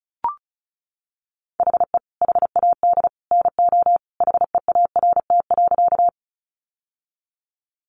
Morse code, heard as a keyed tone, about 34 wpm, 700 Hz. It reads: HE HAD NO HEART.
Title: Sentences from Top 1200 Words 35wpm